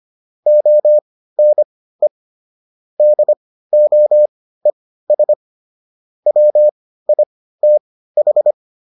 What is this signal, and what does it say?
Morse code, heard as a keyed tone, about 25 wpm, spaced as for 15 wpm, 600 Hz. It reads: ONE DOES WITH
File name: SS215 15